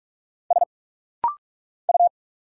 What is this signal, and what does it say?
Morse code, heard as a keyed tone, about 45 wpm, 700 Hz. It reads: S U
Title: Single Letter-Number Alt - Rapid-Fire 45wpm